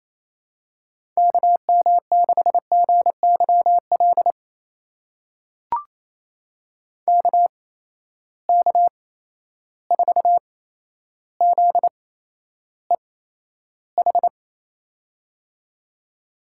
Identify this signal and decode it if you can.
Morse code, heard as a keyed tone, about 28 wpm, spaced as for 12 wpm, 700 Hz. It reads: KM6GYL KK4ZEH
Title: USA Call Signs - 6 Characters - 8x Character Spacing - 28wpm